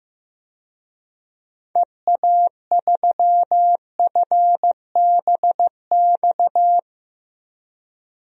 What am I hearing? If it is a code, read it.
Morse code, heard as a keyed tone, about 15 wpm, 700 Hz. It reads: EA3FBX
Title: DXCC-100 Call Signs - 6 Characters - Repeated 2x - 15wpm